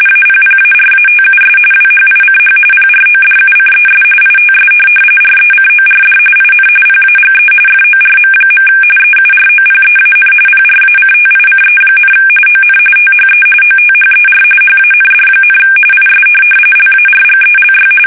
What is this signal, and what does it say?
RTTY, about 45.45 baud. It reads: APPARATUS WHICH PRODUCES ITS RECORD IN PAGE FORM. EUROPEAN PRACTICE, ON THE OTHER HAND, IS LARGELY
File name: fsk850